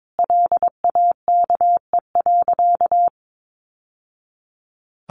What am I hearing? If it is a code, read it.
Morse code, heard as a keyed tone, about 22 wpm, 700 Hz. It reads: LAKE.
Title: Sentences from Top 1100 Words 22wpm